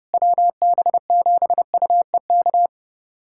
Morse code, heard as a keyed tone, about 32 wpm, 700 Hz. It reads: WB7UEK
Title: USA Callsign DE Callsign 30wpm